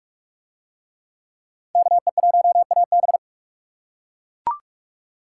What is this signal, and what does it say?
Morse code, heard as a keyed tone, about 40 wpm, 700 Hz. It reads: KE1AB